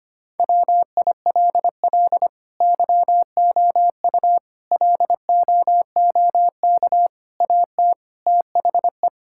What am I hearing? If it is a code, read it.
Morse code, heard as a keyed tone, about 25 wpm, 700 Hz. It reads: WILL YOU LOOK AT THE
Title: Sentences from Top 1000 Words - Review Part 5 - 25wpm